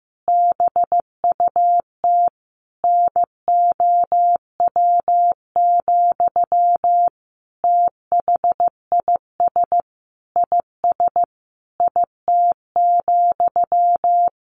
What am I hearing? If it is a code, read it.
Morse code, heard as a keyed tone, about 15 wpm, 700 Hz. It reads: BUT NOW, THIS IS IT,
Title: A Sound Of Thunder 15wpm